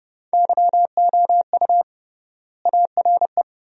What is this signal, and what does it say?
Morse code, heard as a keyed tone, about 30 wpm, 700 Hz. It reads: YOU ARE
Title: 3x Word-Spacing - Sentences from Top 400 Words - 30wpm